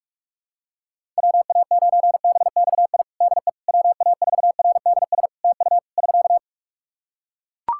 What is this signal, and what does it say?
Morse code, heard as a keyed tone, about 41 wpm, 700 Hz. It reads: WA9BXI DE WA4RDS TU <SK>
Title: Practice QSO - Exchange 4 - Ending - 45wpm